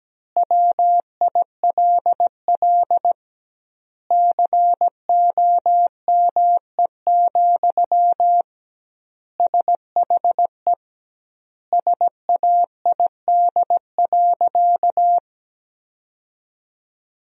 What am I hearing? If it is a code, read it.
Morse code, heard as a keyed tone, about 17 wpm, 700 Hz. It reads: WILL COME, SHE SAID.
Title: 2x Word-Spacing - Sentences from Top 100 Words - 17wpm